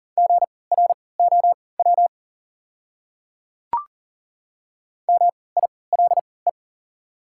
Morse code, heard as a keyed tone, about 40 wpm, spaced as for 23 wpm, 700 Hz. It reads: GROW MILE
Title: 3x Character Spacing - Top 300 Words - 40wpm